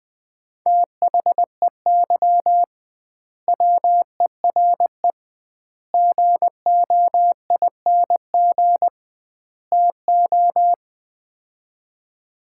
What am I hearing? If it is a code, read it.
Morse code, heard as a keyed tone, about 20 wpm, 700 Hz. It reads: THEY WERE GOING TO
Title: 2x Word-Spacing - Sets of 4 Words - 20wpm